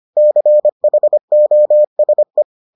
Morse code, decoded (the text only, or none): CHOSE